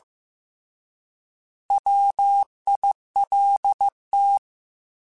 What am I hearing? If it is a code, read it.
Morse code, heard as a keyed tone, about 15 wpm, 800 Hz. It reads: WILT